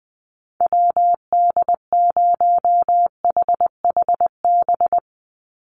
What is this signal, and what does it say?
Morse code, heard as a keyed tone, about 20 wpm, 700 Hz. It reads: WD0HHB
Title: USA Call Signs 20wpm